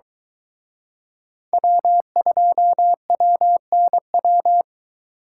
Morse code, heard as a keyed tone, about 22 wpm, 700 Hz. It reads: W2WNW